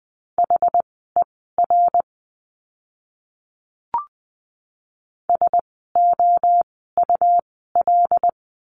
Morse code, heard as a keyed tone, about 20 wpm, spaced as for 15 wpm, 700 Hz. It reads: HER SOUL
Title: 101 Lesson 17f - Letter u - Words 20wpm